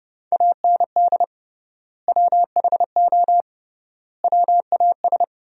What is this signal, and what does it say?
Morse code, heard as a keyed tone, about 30 wpm, 700 Hz. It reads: AND WHO WAS
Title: 3x Word-Spacing - Sentences from Top 100 Words - 30wpm